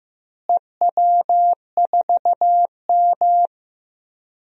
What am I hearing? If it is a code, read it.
Morse code, heard as a keyed tone, about 15 wpm, 700 Hz. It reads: EW4M
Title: DXCC-100 Call Signs - 4 Characters - Repeated 3x - 15wpm